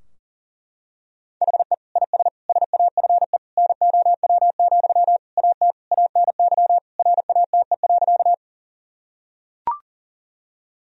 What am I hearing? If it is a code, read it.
Morse code, heard as a keyed tone, about 40 wpm, 700 Hz. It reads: HE IS SAFE NOW, AT ANY RATE.
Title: Sentences from Top 1000 Words - Review Part 4 - 40wpm